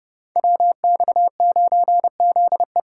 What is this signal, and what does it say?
Morse code, heard as a keyed tone, about 30 wpm, 700 Hz. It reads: WX9ZE
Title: Code Groups - 5 Letter:Numbers - 30wpm